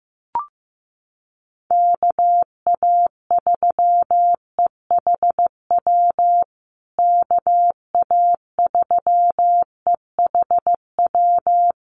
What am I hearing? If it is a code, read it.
Morse code, heard as a keyed tone, about 15 wpm, 700 Hz. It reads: KA3EHW KA3EHW
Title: USA Call Signs Repeated 3x 15wpm